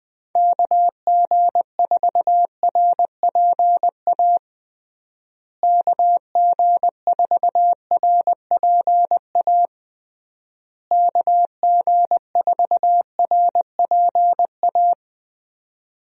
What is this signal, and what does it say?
Morse code, heard as a keyed tone, about 20 wpm, 700 Hz. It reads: KG4RPA KG4RPA KG4RPA
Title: USA Call Signs - 6 Characters - Repeated 3x - 20wpm